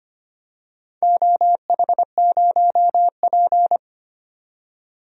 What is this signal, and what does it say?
Morse code, heard as a keyed tone, about 25 wpm, 700 Hz. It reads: OH0P
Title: DXCC-100 Call Signs - 4 Characters - 25wpm